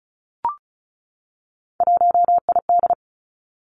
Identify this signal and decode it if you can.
Morse code, heard as a keyed tone, about 36 wpm, 700 Hz. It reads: 1ID
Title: Three Challenging Letter-Number Combinations 35wpm